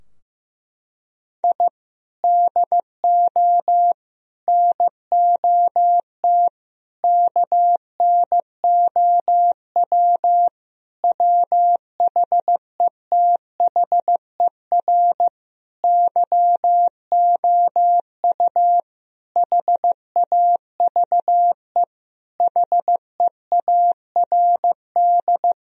Morse code, heard as a keyed tone, about 15 wpm, 700 Hz. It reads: I DO NOT KNOW WHETHER YOU HAVE HEARD
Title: Sentences from Top 1000 Words - Review Part 3 - 15wpm